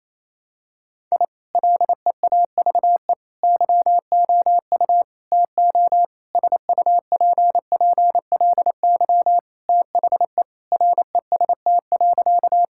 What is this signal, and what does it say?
Morse code, heard as a keyed tone, about 28 wpm, 700 Hz. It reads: I LEAVE YOU TO SUPPLY THE REST.
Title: Sentences from Top 1000 Words - Review Part 5 - 28wpm